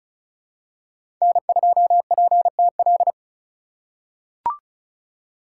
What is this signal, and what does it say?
Morse code, heard as a keyed tone, about 36 wpm, 700 Hz. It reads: N2PTL
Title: USA Call Signs - 2x Character Spacing - 35wpm